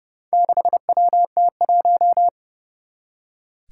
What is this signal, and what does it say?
Morse code, heard as a keyed tone, about 31 wpm, 700 Hz. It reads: 6WT1
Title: Four Challenging Letter-Number Combinations 30wpm